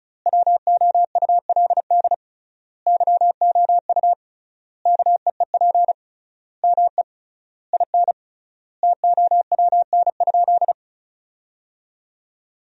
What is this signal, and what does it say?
Morse code, heard as a keyed tone, about 36 wpm, 700 Hz. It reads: WOULD YOU KEEP ME IN TOWN?
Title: 3x Word-Spacing - Sentences from Top 500 Words - 35wpm